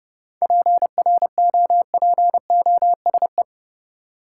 Morse code, heard as a keyed tone, about 32 wpm, 700 Hz. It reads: PROPOSE
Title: Seven Letter Words - 30wpm